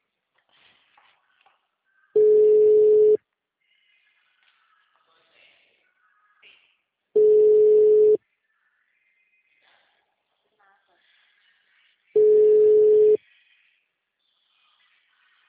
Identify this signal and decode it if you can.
Ringback tone (Europe, 425 Hz), with speech.